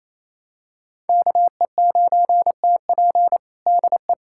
Morse code, heard as a keyed tone, about 28 wpm, 700 Hz. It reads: KE9TP DE
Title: USA Callsign DE Callsign - 5 Characters - 28wpm